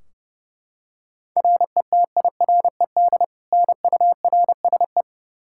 Morse code, heard as a keyed tone, about 32 wpm, 700 Hz. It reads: RETIRED NURSE
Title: Practice QSO - Exchange 3 - More Info - 30wpm